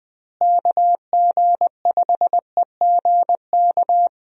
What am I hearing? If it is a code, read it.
Morse code, heard as a keyed tone, about 20 wpm, 700 Hz. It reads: KG5EGK
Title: USA Call Signs - 6 Characters - Repeated 2x - 20wpm